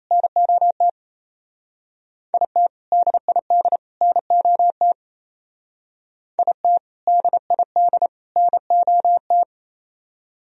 Morse code, heard as a keyed tone, about 32 wpm, 700 Hz. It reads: NOT IT DID NOT IT DID NOT
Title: Speed-Racing - Sets of 3 Words 28wpm